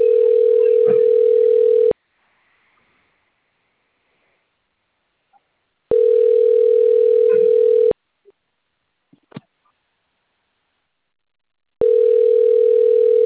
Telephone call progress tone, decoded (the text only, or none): ringback tone